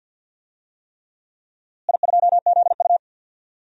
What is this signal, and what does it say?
Morse code, heard as a keyed tone, about 51 wpm, 700 Hz. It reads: I2ZU
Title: DXCC-100 Call Signs - 4 Characters - Repeated 3x - 50wpm